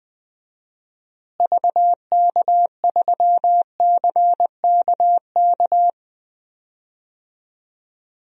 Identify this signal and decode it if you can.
Morse code, heard as a keyed tone, about 20 wpm, 700 Hz. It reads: VK3CKK
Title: DXCC-100 Call Signs - 6 Characters - Repeated 3x - 20wpm